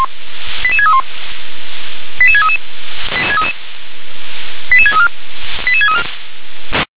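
Five-tone selective call (ZVEI), heard as quick runs of five tones, tones of about 70 ms. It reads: 88631 88633 88634 88634
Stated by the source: Five-tone ZVEI selective call 88510 received by SDR.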